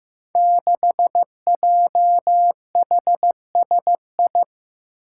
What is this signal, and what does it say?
Morse code, heard as a keyed tone, about 15 wpm, 700 Hz. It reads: 6JHSI
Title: Five Challenging Letter-Number Combinations 15wpm